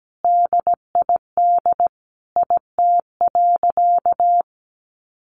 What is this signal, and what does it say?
Morse code, heard as a keyed tone, about 17 wpm, 700 Hz. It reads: DID IT.
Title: Sentences from Top 1200 Words 17wpm